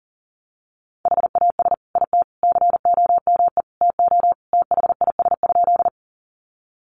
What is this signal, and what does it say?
Morse code, heard as a keyed tone, about 37 wpm, 700 Hz. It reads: HAS IT COME TO THIS?
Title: Sentences from Top 100 Words 40wpm